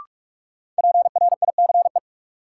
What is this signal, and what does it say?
Morse code, heard as a keyed tone, about 46 wpm, 700 Hz. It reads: PRICE
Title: Sentences from Top 600 Words - New Word Drill - 45wpm